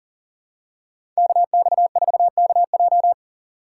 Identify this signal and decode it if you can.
Morse code, heard as a keyed tone, about 40 wpm, 700 Hz. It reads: KX4KJ
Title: USA Call Signs - 5 Characters - 2x Character Spacing - 40wpm